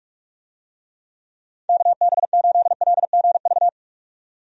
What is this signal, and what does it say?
Morse code, heard as a keyed tone, about 41 wpm, 700 Hz. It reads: KB8LGV
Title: USA Call Signs - 6 Characters - 45wpm